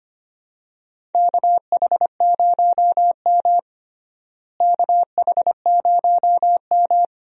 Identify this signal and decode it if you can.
Morse code, heard as a keyed tone, about 25 wpm, 700 Hz. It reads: KH0M KH0M
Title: USA Call Signs - 4 Characters - Repeated 2x - 25wpm